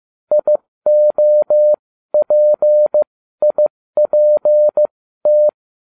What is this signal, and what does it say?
Morse code, heard as a keyed tone, about 15 wpm, spaced as for 13 wpm, 600 Hz. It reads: IOPIPT